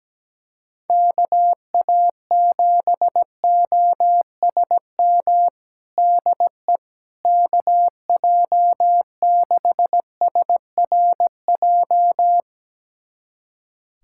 Morse code, heard as a keyed tone, about 17 wpm, 700 Hz. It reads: KA7OSM DE KJ6SRJ
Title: USA Callsign DE Callsign 17wpm